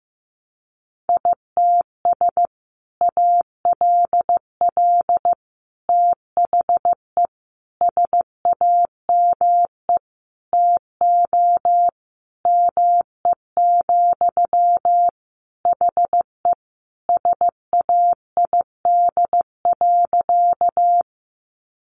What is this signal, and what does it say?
Morse code, heard as a keyed tone, about 15 wpm, 700 Hz. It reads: ITS ALL THE SAME TO ME, HE SAID.
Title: Sentences from Top 600 Words 15wpm